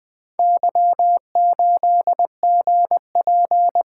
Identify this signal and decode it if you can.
Morse code, heard as a keyed tone, about 20 wpm, 700 Hz. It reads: Y8GP
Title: Four Letter-Number Combinations - 20wpm